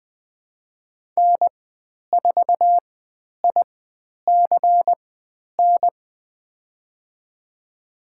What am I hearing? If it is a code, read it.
Morse code, heard as a keyed tone, about 20 wpm, spaced as for 10 wpm, 700 Hz. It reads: N4ICN
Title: 095 Lesson 16g - Letter c - Call Signs 20wpm